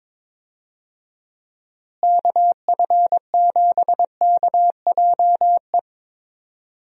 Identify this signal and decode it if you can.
Morse code, heard as a keyed tone, about 22 wpm, 700 Hz. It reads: KF7KJE
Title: USA Call Signs - 6 Characters - Repeated 2x - 22wpm